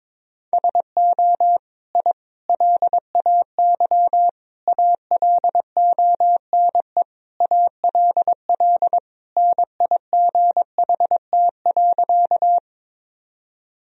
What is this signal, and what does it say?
Morse code, heard as a keyed tone, about 21 wpm, 700 Hz. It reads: SO I LAY ALONE ALL NIGHT.
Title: Sentences from Top 1000 Words - Review Part 2 - 22wpm